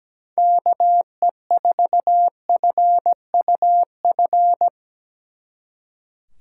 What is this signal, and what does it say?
Morse code, heard as a keyed tone, about 20 wpm, 700 Hz. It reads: KE4FUF
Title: Speed Racing - USA Call Signs 17wpm